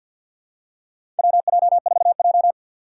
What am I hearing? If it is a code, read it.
Morse code, heard as a keyed tone, about 52 wpm, 700 Hz. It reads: WJ4J
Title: USA Call Signs - 4 Characters - 3x Character Spacing - 50wpm